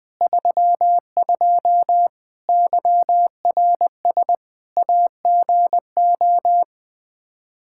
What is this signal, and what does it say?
Morse code, heard as a keyed tone, about 20 wpm, 700 Hz. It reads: 32 YRS AGO
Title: Practice QSO - Exchange 3 - More Info - 20wpm